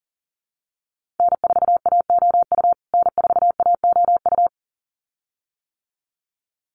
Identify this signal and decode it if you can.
Morse code, heard as a keyed tone, about 40 wpm, 700 Hz. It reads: N4AOU N4AOU
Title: USA Call Signs Repeated 2x 40wpm